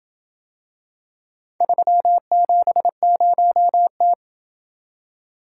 Morse code, heard as a keyed tone, about 26 wpm, 700 Hz. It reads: 370T